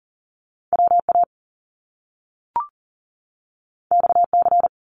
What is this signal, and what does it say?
Morse code, heard as a keyed tone, about 40 wpm, 700 Hz. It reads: WA XC
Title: Top Two Letter Combinations in Words 40wpm